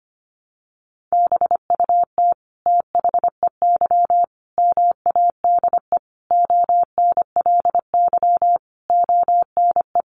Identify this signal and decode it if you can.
Morse code, heard as a keyed tone, about 25 wpm, 700 Hz. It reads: BUT THEY MADE ONLY ONE
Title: Sentences from Top 1300 Words 25wpm